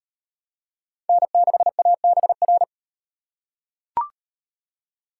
Morse code, heard as a keyed tone, about 34 wpm, 700 Hz. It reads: N6ABR